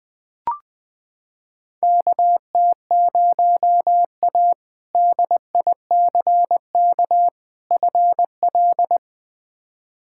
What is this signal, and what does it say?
Morse code, heard as a keyed tone, about 20 wpm, 700 Hz. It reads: KT0A DICK FL
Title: K1USN SST QSO Practice - 20wpm